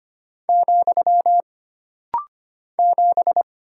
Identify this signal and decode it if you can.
Morse code, heard as a keyed tone, about 25 wpm, 700 Hz. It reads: , 7
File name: Single Character (Letter:Number:Punctuation:Prosign) - Rapid-Fire - 25wpm